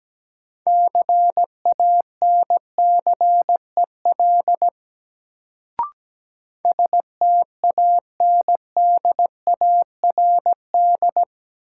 Morse code, heard as a keyed tone, about 17 wpm, 700 Hz. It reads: CANCEL STANDARD
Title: Sentences from Top 1200 Words - New Word Drill Alt - 17wpm